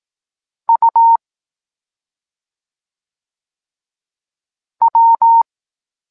Morse code, heard as a keyed tone, about 18 wpm, 920 Hz. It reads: U W
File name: Tekst-26 1